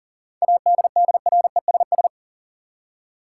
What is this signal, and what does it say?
Morse code, heard as a keyed tone, about 40 wpm, 700 Hz. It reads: ADDRESS
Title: Seven Letter Words - 40wpm